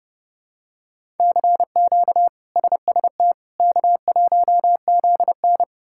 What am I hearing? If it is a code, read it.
Morse code, heard as a keyed tone, about 30 wpm, 700 Hz. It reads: CQ SST K1ZN
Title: K1USN SST QSO Practice - 30wpm